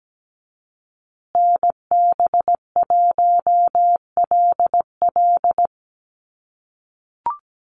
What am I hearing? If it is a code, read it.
Morse code, heard as a keyed tone, about 17 wpm, 700 Hz. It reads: NB1LL
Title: USA Call Signs Repeated 3x 17wpm